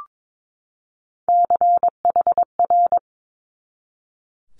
Morse code, heard as a keyed tone, about 22 wpm, 700 Hz. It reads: CHR
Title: Multi-letter Phonemes 22wpm